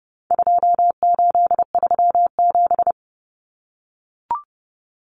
Morse code, heard as a keyed tone, about 30 wpm, 700 Hz. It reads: 2837
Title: Common Cut Numbers - 4-Digit 30wpm